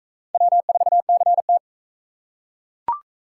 Morse code, heard as a keyed tone, about 41 wpm, 700 Hz. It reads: W4CT